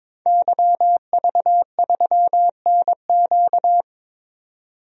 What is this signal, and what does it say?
Morse code, heard as a keyed tone, about 22 wpm, 700 Hz. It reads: YV3NQ
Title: DXCC-100 Call Signs - 5 Characters - 22wpm